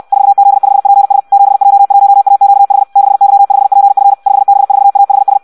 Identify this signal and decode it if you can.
Morse code, heard as a keyed tone, about 64 wpm, 800 Hz. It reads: 4Z5PS 7J1AQH 6Y5PF 5B4AHS
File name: Audio practica CW / qrq-tx-66 2